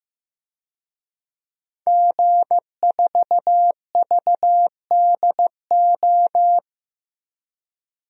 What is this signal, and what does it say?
Morse code, heard as a keyed tone, about 15 wpm, 700 Hz. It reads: G4VDO
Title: DXCC-100 Call Signs - 5 Characters - Repeated 3x - 15wpm